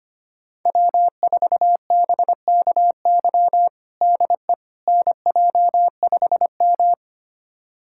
Morse code, heard as a keyed tone, about 25 wpm, 700 Hz. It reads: W4BKY DE NJ5M
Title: USA Callsign DE Callsign 25wpm